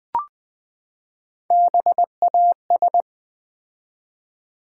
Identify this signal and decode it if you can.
Morse code, heard as a keyed tone, about 20 wpm, 700 Hz. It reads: BAS
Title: Build A Word - 3 Letters At A Time - 8-Letter Words - 20wpm